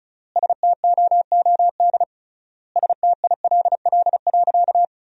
Morse code, heard as a keyed tone, about 34 wpm, 700 Hz. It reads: STOOD STILL.
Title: 3x Word-Spacing - Sentences from Top 500 Words - 35wpm